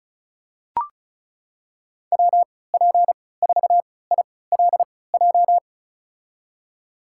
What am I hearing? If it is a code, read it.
Morse code, heard as a keyed tone, about 36 wpm, spaced as for 20 wpm, 700 Hz. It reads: WP4ILJ
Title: USA Call Signs - 6 Characters - 3x Character Spacing - 35wpm